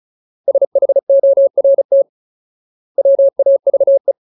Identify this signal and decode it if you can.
Morse code, heard as a keyed tone, about 34 wpm, spaced as for 25 wpm, 550 Hz. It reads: SHORTWAVE
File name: Ultimate Word List / 35WPM Ultimate